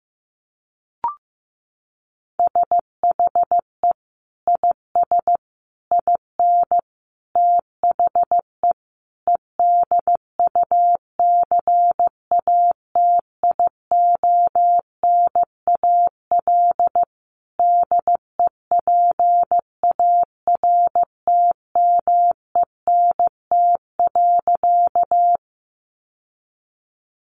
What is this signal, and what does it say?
Morse code, heard as a keyed tone, about 15 wpm, 700 Hz. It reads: SHE IS IN THE EDUCATIONAL DEPARTMENT.
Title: Sentences from Top 1200 Words 15wpm